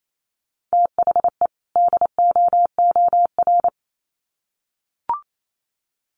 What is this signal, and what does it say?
Morse code, heard as a keyed tone, about 28 wpm, 700 Hz. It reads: THE DOOR
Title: Sets of 2 Words 28wpm